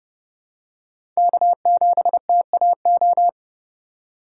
Morse code, heard as a keyed tone, about 30 wpm, 700 Hz. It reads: K7TAO
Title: USA Call Signs - 5 Characters - 4x Character Spacing - 30wpm